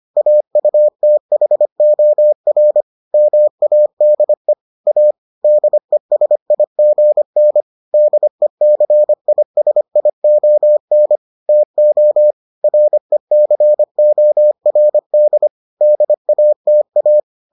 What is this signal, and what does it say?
Morse code, heard as a keyed tone, about 25 wpm, 600 Hz. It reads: AUTHOR MADE A DESIGN DECISION TO RECORD DATA